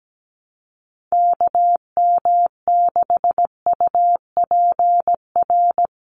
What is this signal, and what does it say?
Morse code, heard as a keyed tone, about 17 wpm, 700 Hz. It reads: KM6UPR
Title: USA Call Signs 17wpm